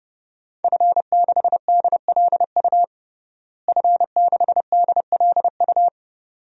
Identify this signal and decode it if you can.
Morse code, heard as a keyed tone, about 30 wpm, 700 Hz. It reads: F6DLU F6DLU
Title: DXCC-100 Call Signs - 5 Characters - Repeated 2x - 30wpm